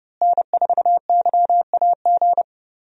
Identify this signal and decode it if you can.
Morse code, heard as a keyed tone, about 32 wpm, 700 Hz. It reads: N4YAG
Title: USA Call Signs - 5 Characters - Repeated 3x - 30wpm